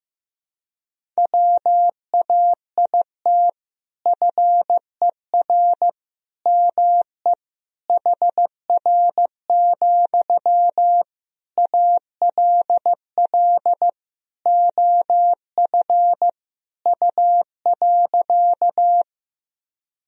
Morse code, heard as a keyed tone, about 15 wpm, 700 Hz. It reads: WAIT FER ME HR, ALL OF U.
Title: Sentences from Top 500 Words and Abbreviations 15wpm